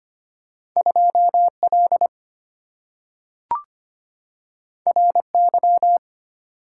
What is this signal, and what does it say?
Morse code, heard as a keyed tone, about 25 wpm, 700 Hz. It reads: 2L RY
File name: Two Letter-Number Combinations - 25wpm